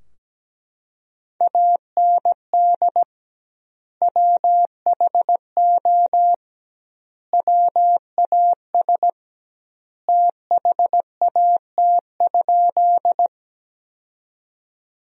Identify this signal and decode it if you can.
Morse code, heard as a keyed tone, about 17 wpm, 700 Hz. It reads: AND WHO WAS THAT?
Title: 2x Word-Spacing - Sentences from Top 100 Words - 17wpm